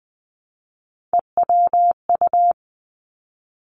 Morse code, heard as a keyed tone, about 20 wpm, 700 Hz. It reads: EWU
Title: Three Challenging Letter-Number Combinations 20wpm